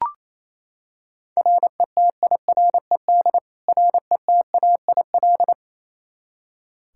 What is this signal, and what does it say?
Morse code, heard as a keyed tone, about 28 wpm, 700 Hz. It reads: RETIRED RETAIL
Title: K1USN SST QSO Practice 28wpm